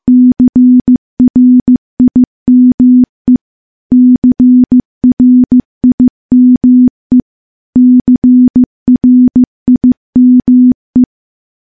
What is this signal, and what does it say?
Morse code, heard as a keyed tone, about 15 wpm, 250 Hz. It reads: CRIME CRIME CRIME
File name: morse / morse-training-15WPM